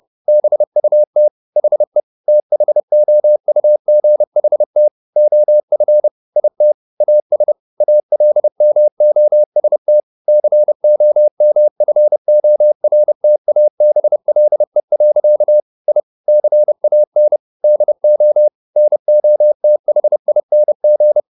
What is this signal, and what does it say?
Morse code, heard as a keyed tone, about 30 wpm, 600 Hz. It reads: BUT HE THOUGHT OF IT AS ALMOST COMFORTABLE. I CAN DO NOTHING
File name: the old man and the sea / the old man and the sea 1 30wpm